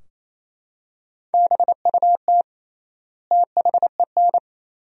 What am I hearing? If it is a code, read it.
Morse code, heard as a keyed tone, about 27 wpm, 700 Hz. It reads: BUT THEN
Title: 3x Word-Spacing - Sentences from Top 300 Words - 28wpm